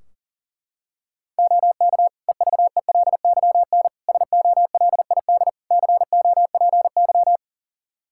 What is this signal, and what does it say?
Morse code, heard as a keyed tone, about 40 wpm, 700 Hz. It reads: OK EVELYN SOLID COPY
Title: Practice QSO - Exchange 3 - More Info - 40wpm